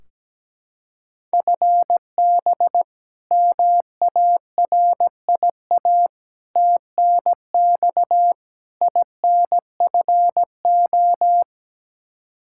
Morse code, heard as a keyed tone, about 17 wpm, 700 Hz. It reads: FB MARIA TNX INFO
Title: Practice QSO - Exchange 3 - More Info - 17wpm